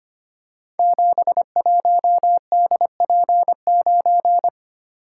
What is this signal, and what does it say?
Morse code, heard as a keyed tone, about 26 wpm, 700 Hz. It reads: 71DP9
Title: Code Groups - 5 Letter:Numbers - 25wpm